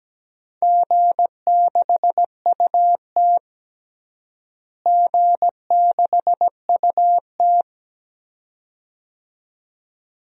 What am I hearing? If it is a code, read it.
Morse code, heard as a keyed tone, about 17 wpm, 700 Hz. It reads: G6UT G6UT
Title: DXCC-100 Call Signs - 4 Characters - Repeated 2x - 17wpm